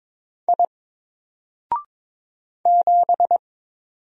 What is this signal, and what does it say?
Morse code, heard as a keyed tone, about 22 wpm, 700 Hz. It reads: I 7
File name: Single-Letter-Number-22wpm